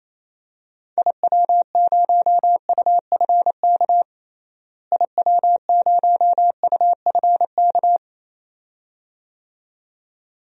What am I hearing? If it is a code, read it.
Morse code, heard as a keyed tone, about 28 wpm, 700 Hz. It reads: IW0UFK IW0UFK
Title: DXCC-100 Call Signs - 6 Characters - Repeated 2x - 28wpm